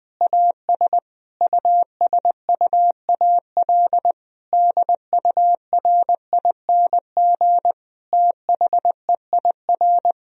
Morse code, heard as a keyed tone, about 21 wpm, 700 Hz. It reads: AS USUAL DURING THEIR
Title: Sentences from Top 1000 Words - Review Part 2 - 20wpm